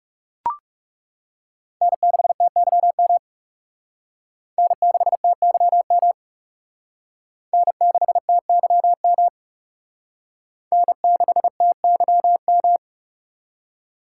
Morse code, heard as a keyed tone, about 39 wpm, 700 Hz. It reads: N6TYM N6TYM N6TYM N6TYM